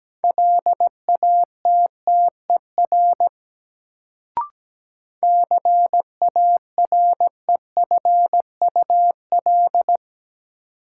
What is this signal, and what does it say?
Morse code, heard as a keyed tone, about 17 wpm, 700 Hz. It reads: LATTER CAREFUL
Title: Sentences from Top 1600 Words - New Word Drill - 17wpm